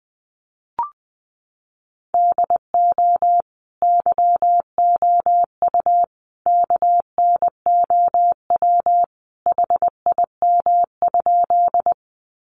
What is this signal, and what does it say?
Morse code, heard as a keyed tone, about 20 wpm, 700 Hz. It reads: DO YOU KNOW HIM?
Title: Sentences from Top 100 Words 20wpm